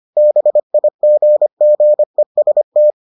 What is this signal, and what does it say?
Morse code, heard as a keyed tone, about 25 wpm, 600 Hz. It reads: BIGGEST